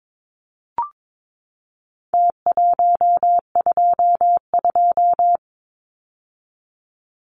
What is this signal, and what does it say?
Morse code, heard as a keyed tone, about 22 wpm, 700 Hz. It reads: T122
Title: Numbers - Contest Style - 4-Digit 22wpm